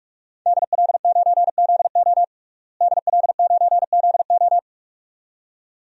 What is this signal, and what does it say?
Morse code, heard as a keyed tone, about 46 wpm, 700 Hz. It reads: DL9ZO DL9ZO
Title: DXCC-100 Call Signs - 5 Characters - Repeated 2x - 45wpm